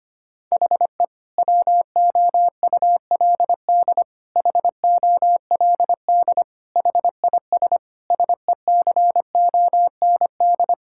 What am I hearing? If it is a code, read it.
Morse code, heard as a keyed tone, about 25 wpm, 700 Hz. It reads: HE WOULD HOLD HIS SECOND
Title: Sentences from Top 1000 Words - Review Part 6 - 25wpm